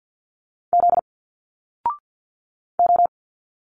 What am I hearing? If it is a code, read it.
Morse code, heard as a keyed tone, about 50 wpm, 700 Hz. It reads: Z O
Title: Single Letters 50wpm